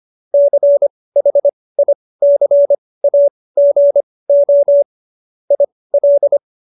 Morse code, heard as a keyed tone, about 25 wpm, spaced as for 18 wpm, 570 Hz. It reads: CHICAGO IL
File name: qso222 18